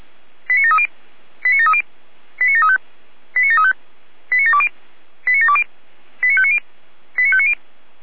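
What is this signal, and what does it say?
Five-tone selective call (ZVEI), heard as quick runs of five tones, tones of about 70 ms. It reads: 89730 89730 89735 89735 89720 89720 89590 89590
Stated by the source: ZVEI five-tone selective call 89705 received via SDR.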